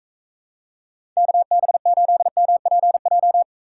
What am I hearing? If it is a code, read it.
Morse code, heard as a keyed tone, about 45 wpm, 700 Hz. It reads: KB8MPJ